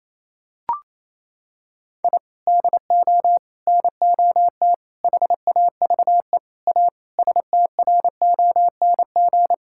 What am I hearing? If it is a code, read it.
Morse code, heard as a keyed tone, about 28 wpm, 700 Hz. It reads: I DO NOT HAVE A STRONG
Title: Sentences from Top 1000 Words - Review Part 5 - 28wpm